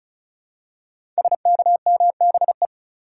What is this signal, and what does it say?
Morse code, heard as a keyed tone, about 36 wpm, 700 Hz. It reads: SKMBE
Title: Code Groups - 5 Letters - 35wpm